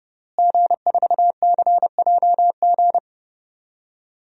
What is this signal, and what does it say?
Morse code, heard as a keyed tone, about 30 wpm, 700 Hz. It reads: G4CJG
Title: DXCC-100 Call Signs - 5 Characters - Repeated 2x - 30wpm